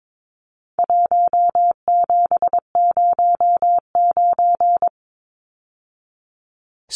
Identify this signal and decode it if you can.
Morse code, heard as a keyed tone, about 22 wpm, 700 Hz. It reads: 1709
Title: Numbers - Contest Style - 4-Digit 22wpm